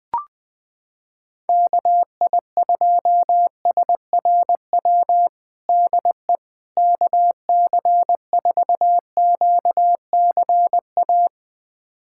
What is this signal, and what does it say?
Morse code, heard as a keyed tone, about 20 wpm, 700 Hz. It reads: KI2SRW DE KC4QCA
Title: USA Callsign DE Callsign 20wpm